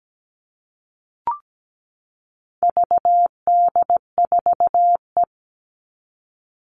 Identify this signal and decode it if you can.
Morse code, heard as a keyed tone, about 17 wpm, 700 Hz. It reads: VD4E
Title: All Cut Numbers - 4-Digit 17wpm